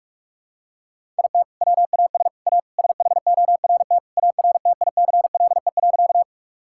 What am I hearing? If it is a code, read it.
Morse code, heard as a keyed tone, about 41 wpm, 700 Hz. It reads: IT WAS A SHORT ARTICLE.
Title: Sentences from Top 1000 Words - Review Part 5 - 45wpm